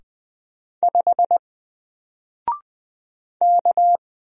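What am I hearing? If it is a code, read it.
Morse code, heard as a keyed tone, about 20 wpm, 700 Hz. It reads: 5 K
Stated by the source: Single letter and number Morse drill at 20 wpm, from N4LCD.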